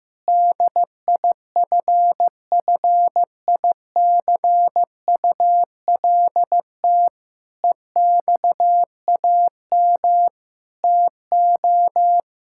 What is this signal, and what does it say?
Morse code, heard as a keyed tone, about 14 wpm, 700 Hz. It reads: DIFFICULT EXAM TO
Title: Sentences from Top 1000 Words - Review Part 5 - 15wpm